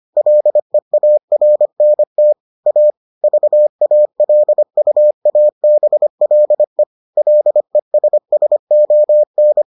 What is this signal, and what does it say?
Morse code, heard as a keyed tone, about 25 wpm, 600 Hz. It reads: LEARNT A VALUABLE LESSON